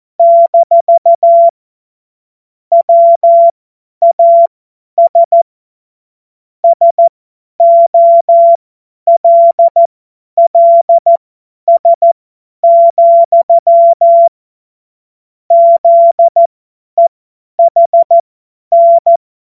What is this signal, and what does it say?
Morse code, heard as a keyed tone, about 14 wpm, spaced as for 10 wpm, 680 Hz. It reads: - WAS SOLLS, ZEHN
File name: Morsekurs / Witze / Witz 26. April 2022 50 F70